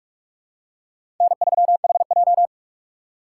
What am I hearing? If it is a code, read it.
Morse code, heard as a keyed tone, about 46 wpm, 700 Hz. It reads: N3HJ
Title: USA Call Signs - 4x Character Spacing - 45wpm